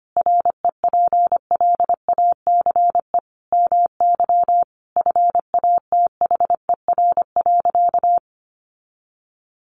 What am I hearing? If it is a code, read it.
Morse code, heard as a keyed tone, about 25 wpm, 700 Hz. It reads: REPLACE MY FATHER.
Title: Sentences from Top 1400 Words 25wpm